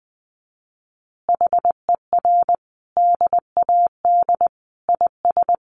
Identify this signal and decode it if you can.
Morse code, heard as a keyed tone, about 20 wpm, 700 Hz. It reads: HER DAD IS
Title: Additional Easy Phrases 20wpm